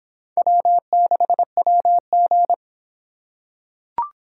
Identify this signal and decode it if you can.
Morse code, heard as a keyed tone, about 25 wpm, 700 Hz. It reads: W6WG